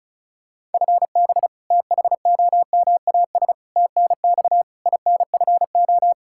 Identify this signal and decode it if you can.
Morse code, heard as a keyed tone, about 36 wpm, 700 Hz. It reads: FB THOMAS TNX INFO
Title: Practice QSO - Exchange 3 - More Info - 35wpm